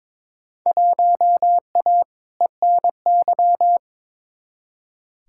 Morse code, heard as a keyed tone, about 22 wpm, 700 Hz. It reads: 1A ENY
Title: Field Day Exchanges - 22wpm